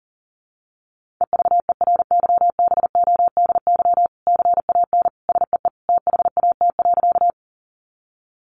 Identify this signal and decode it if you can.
Morse code, heard as a keyed tone, about 37 wpm, 700 Hz. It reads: EVERYBODY CAN SEE THAT.
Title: Sentences from Top 1400 Words 40wpm